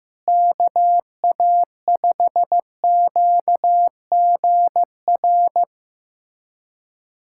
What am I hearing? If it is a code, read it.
Morse code, heard as a keyed tone, about 15 wpm, 700 Hz. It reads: KA5QGR
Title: USA Callsign DE Callsign 15wpm